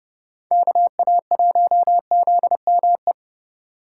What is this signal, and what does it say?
Morse code, heard as a keyed tone, about 30 wpm, 700 Hz. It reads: KA1ZME